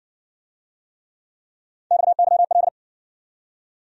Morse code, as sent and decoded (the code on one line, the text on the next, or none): -..- -.-- .-..
XYL